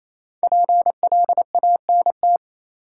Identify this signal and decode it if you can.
Morse code, heard as a keyed tone, about 28 wpm, 700 Hz. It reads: PLANT
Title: Sentences from Top 1000 Words - Review Part 1 - New Word Drill - 28wpm